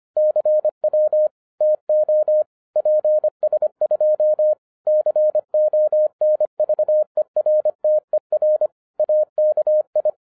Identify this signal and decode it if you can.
Morse code, heard as a keyed tone, about 25 wpm, 610 Hz. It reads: CW TO PS2 CONVERTER AKI
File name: 2011-10-mp3 / n09-CW TO PS2 CONVERTER 25wpm